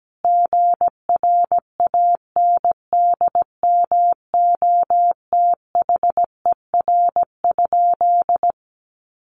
Morse code, heard as a keyed tone, about 17 wpm, 700 Hz. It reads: GRANDMOTHER?
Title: Sentences from Top 1200 Words 17wpm